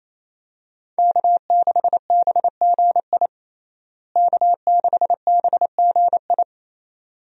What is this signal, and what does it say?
Morse code, heard as a keyed tone, about 28 wpm, 700 Hz. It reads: K6BGI K6BGI
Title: USA Call Signs - 5 Characters - Repeated 2x - 28wpm